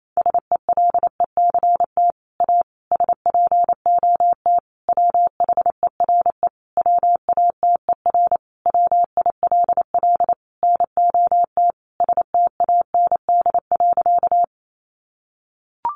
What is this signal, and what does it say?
Morse code, heard as a keyed tone, about 28 wpm, 700 Hz. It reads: SELECT A SPOT WHERE WATER WILL NOT STAND.
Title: Sentences from Top 1300 Words 28wpm